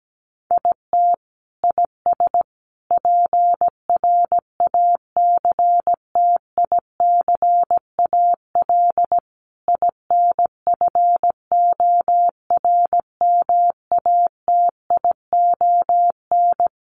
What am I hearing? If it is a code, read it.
Morse code, heard as a keyed tone, about 17 wpm, 700 Hz. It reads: IT IS PRACTICAL INFORMATION
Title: Sentences from Top 1400 Words 17wpm